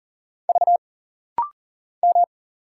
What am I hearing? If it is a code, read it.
Morse code, heard as a keyed tone, about 40 wpm, 700 Hz. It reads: V M
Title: Single Letters - Rapid-Fire 40wpm